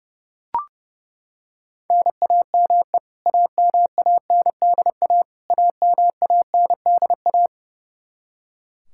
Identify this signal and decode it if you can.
Morse code, heard as a keyed tone, about 30 wpm, 700 Hz. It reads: NAME AMANDA AMANDA
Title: K1USN SST QSO Practice 30wpm